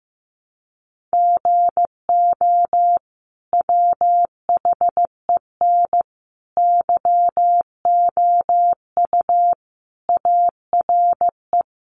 Morse code, heard as a keyed tone, about 15 wpm, 700 Hz. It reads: GO WHEN YOU ARE
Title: Sentences from Top 500 Words 15wpm